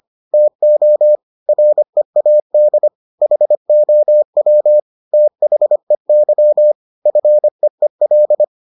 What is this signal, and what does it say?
Morse code, heard as a keyed tone, about 25 wpm, 600 Hz. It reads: TO READ HOW THEY FEEL